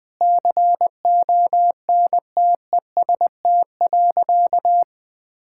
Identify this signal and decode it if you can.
Morse code, heard as a keyed tone, about 20 wpm, 700 Hz. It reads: CONTEST.
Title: Sentences from Top 1600 Words 20wpm